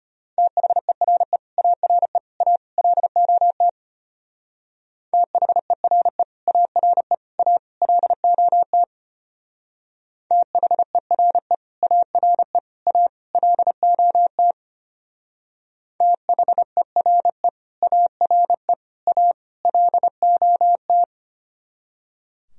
Morse code, heard as a keyed tone, about 31 wpm, 700 Hz. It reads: THERE ARE A LOT THERE ARE A LOT THERE ARE A LOT THERE ARE A LOT
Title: Speed-Racing - Sets of 4 Words 25wpm